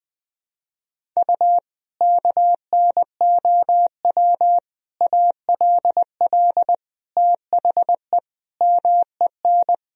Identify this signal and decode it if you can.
Morse code, heard as a keyed tone, about 21 wpm, 700 Hz. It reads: U KNOW ALL THE MEN
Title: Sentences from Top 300 Words and Abbreviations 20wpm